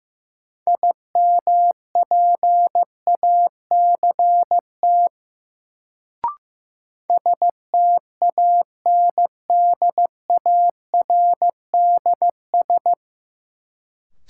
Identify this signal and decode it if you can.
Morse code, heard as a keyed tone, about 15 wpm, 700 Hz. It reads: IMPACT STANDARDS
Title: Sentences from Top 1000 Words - Review Part 6 - New Word Drill - 15wpm